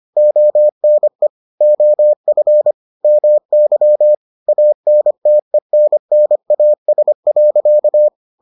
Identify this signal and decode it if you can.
Morse code, heard as a keyed tone, about 25 wpm, 600 Hz. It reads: ONE OF MY ANTENNAS.